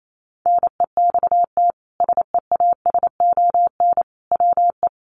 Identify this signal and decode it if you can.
Morse code, heard as a keyed tone, about 28 wpm, 700 Hz. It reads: NEXT SEASON WE
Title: Sentences from Top 1400 Words 28wpm